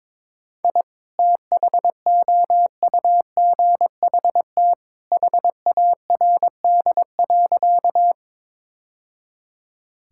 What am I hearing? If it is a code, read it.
Morse code, heard as a keyed tone, about 22 wpm, 700 Hz. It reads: I THOUGHT HARD.
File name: practice / 22wpm / Sentences with 2-Word Phrases - 22wpm